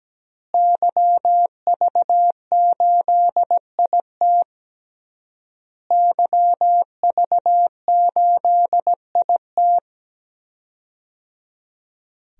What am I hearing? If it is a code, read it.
Morse code, heard as a keyed tone, about 17 wpm, 700 Hz. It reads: YV8IT YV8IT
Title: DXCC-100 Call Signs - 5 Characters - Repeated 3x - 17wpm